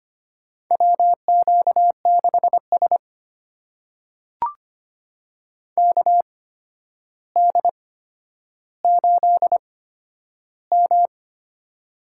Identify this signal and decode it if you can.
Morse code, heard as a keyed tone, about 25 wpm, spaced as for 11 wpm, 700 Hz. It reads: WQ6S KD8M
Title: USA Call Signs - 4 Characters - 8x Character Spacing - 25wpm